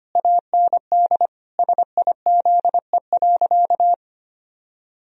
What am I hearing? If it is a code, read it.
Morse code, heard as a keyed tone, about 25 wpm, 700 Hz. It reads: AND SIZE.
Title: Sentences from Top 1000 Words - Review Part 3 - 25wpm